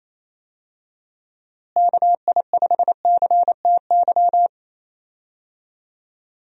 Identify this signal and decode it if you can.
Morse code, heard as a keyed tone, about 28 wpm, 700 Hz. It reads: KI5CTY
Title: USA Call Signs - 6 Characters - Repeated 3x - 28wpm